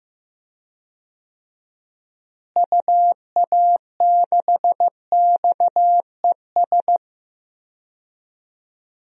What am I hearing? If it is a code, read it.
Morse code, heard as a keyed tone, about 15 wpm, 700 Hz. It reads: UA6XES
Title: DXCC-100 Call Signs - 6 Characters - Repeated 2x - 15wpm